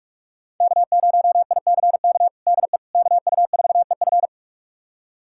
Morse code, heard as a keyed tone, about 46 wpm, 700 Hz. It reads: K0ICK DE KU4EF
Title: USA Callsign DE Callsign - 5 Characters - 45wpm